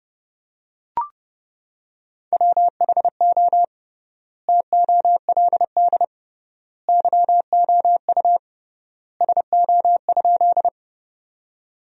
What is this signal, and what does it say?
Morse code, heard as a keyed tone, about 30 wpm, 700 Hz. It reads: WHO TOLD YOU SO?
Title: 3x Word-Spacing - Sentences from Top 400 Words - 30wpm